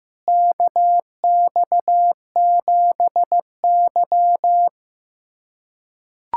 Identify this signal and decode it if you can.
Morse code, heard as a keyed tone, about 15 wpm, 700 Hz. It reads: KX7Y
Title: USA Callsign DE Callsign - 4 Characters - 15wpm